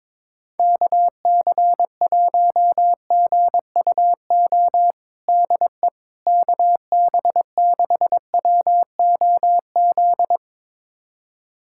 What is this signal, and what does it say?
Morse code, heard as a keyed tone, about 22 wpm, 700 Hz. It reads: KC1GUO DE KB6WOZ
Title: USA Callsign DE Callsign 22wpm